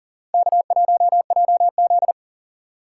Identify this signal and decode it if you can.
Morse code, heard as a keyed tone, about 52 wpm, 700 Hz. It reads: K1JZ